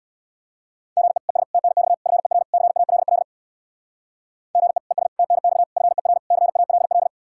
Morse code, heard as a keyed tone, about 62 wpm, 700 Hz. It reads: GE ES TNX FER CALL GE ES TNX FER CALL
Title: Speed-Racing Common QSO Phrases - 50wpm